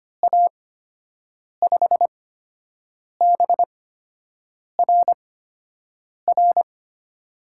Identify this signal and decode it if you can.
Morse code, heard as a keyed tone, about 25 wpm, spaced as for 7 wpm, 700 Hz. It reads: A5BRR
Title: USA Call Signs - 8x Character Spacing - 25wpm